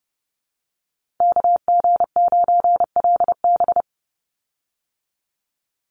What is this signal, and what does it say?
Morse code, heard as a keyed tone, about 30 wpm, 700 Hz. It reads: KG9LB
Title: Call Signs Repeated 2x - 30wpm